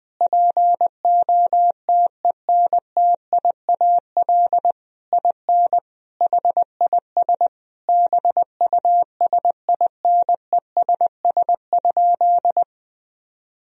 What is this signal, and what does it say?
Morse code, heard as a keyed tone, about 20 wpm, 700 Hz. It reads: POTENTIAL IN HIS BUSINESS?
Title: Sentences from Top 1000 Words - Review Part 4 - 20wpm